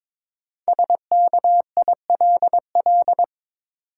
Morse code, heard as a keyed tone, about 22 wpm, 690 Hz. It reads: SKILL
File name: 5 Letter Words - First 3 Letters Given - 22wpm